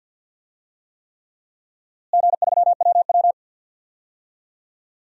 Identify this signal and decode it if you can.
Morse code, heard as a keyed tone, about 50 wpm, 700 Hz. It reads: G3WW
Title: DXCC-100 Call Signs - 4 Characters - Repeated 3x - 50wpm